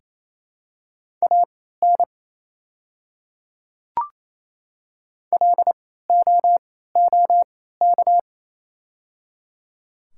Morse code, heard as a keyed tone, about 28 wpm, spaced as for 16 wpm, 700 Hz. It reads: AN LOOK
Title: 3x Character Spacing - Top 100 Words - 28wpm